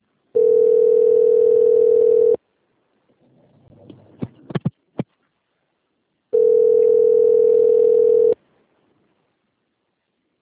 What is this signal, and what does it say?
Ringback tone (North America), 440 + 480 Hz.